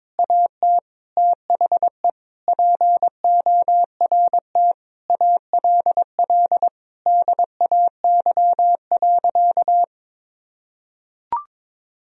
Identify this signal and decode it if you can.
Morse code, heard as a keyed tone, about 22 wpm, 700 Hz. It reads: AT THE PORT ALL DAY.
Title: Sentences from Top 1000 Words - Review Part 1 - 22wpm